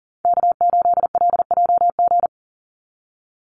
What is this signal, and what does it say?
Morse code, heard as a keyed tone, about 40 wpm, 700 Hz. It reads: K8LJG
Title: NA SOTA Chaser Callsigns 40wpm